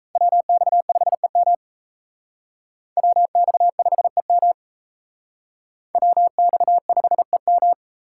Morse code, heard as a keyed tone, about 36 wpm, 700 Hz. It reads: WX5EM WX5EM WX5EM